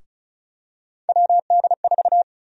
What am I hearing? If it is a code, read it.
Morse code, heard as a keyed tone, about 34 wpm, 700 Hz. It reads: WD4
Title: USA Call Sign Prefixes - Repeated 2x - 35wpm